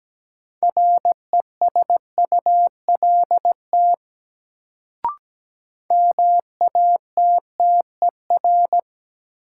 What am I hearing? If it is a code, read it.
Morse code, heard as a keyed tone, about 17 wpm, 700 Hz. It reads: RESULT MATTER
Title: Sentences from Top 700 Words - New Word Drill - 17wpm